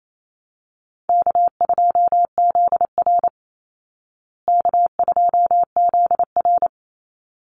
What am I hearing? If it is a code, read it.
Morse code, heard as a keyed tone, about 28 wpm, 700 Hz. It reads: K2ZR K2ZR
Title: Easy Call Signs Repeated 2x - 28wpm